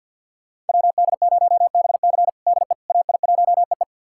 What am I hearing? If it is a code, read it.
Morse code, heard as a keyed tone, about 47 wpm, 700 Hz. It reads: WD0BC DE AI1EE
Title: USA Callsign DE Callsign - 5 Characters - 50wpm